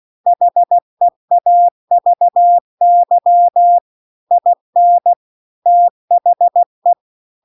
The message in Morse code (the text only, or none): HEAVY IN THE